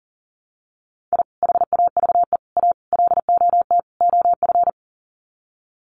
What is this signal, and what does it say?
Morse code, heard as a keyed tone, about 40 wpm, 700 Hz. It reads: I HAVE A LOT OF
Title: Sets of 5 Words 40wpm